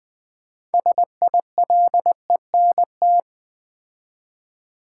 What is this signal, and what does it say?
Morse code, heard as a keyed tone, about 20 wpm, 700 Hz. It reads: SILENT
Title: Build A Word - 2 Letters At A Time - 6-Letter Words - 20wpm